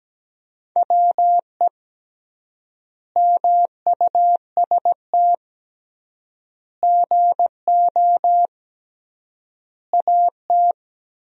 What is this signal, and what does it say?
Morse code, heard as a keyed tone, about 17 wpm, 700 Hz. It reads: WE MUST GO AT
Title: 3x Word-Spacing - Sentences from Top 300 Words - 17wpm